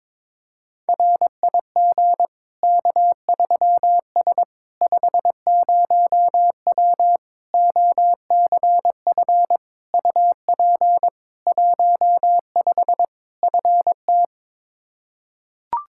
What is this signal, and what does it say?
Morse code, heard as a keyed tone, about 22 wpm, 700 Hz. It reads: RIG K3S 50W OCF UP 15 FT
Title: K1USN SST QSO Practice 22wpm